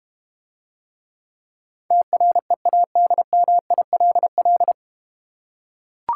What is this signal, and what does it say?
Morse code, heard as a keyed tone, about 32 wpm, 700 Hz. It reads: TREADMILL